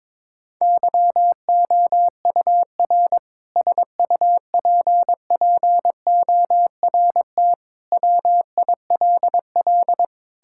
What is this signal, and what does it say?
Morse code, heard as a keyed tone, about 22 wpm, 700 Hz. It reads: YOUR SUPPORT WILL
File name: Sentences with 7-Word Phrases - 22wpm